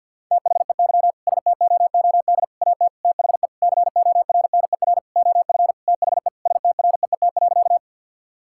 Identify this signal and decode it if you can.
Morse code, heard as a keyed tone, about 50 wpm, 700 Hz. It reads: THEY STOOD AT THE CORNER OF THE STREET.
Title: Sentences from Top 1600 Words 50wpm